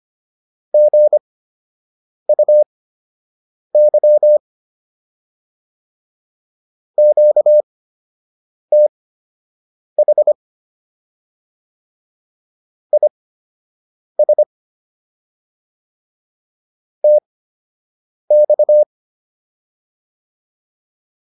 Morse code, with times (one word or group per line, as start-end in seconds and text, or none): 0.7-4.4 GUY
7.0-10.3 QTH
12.9-14.4 IS
17.0-18.8 TX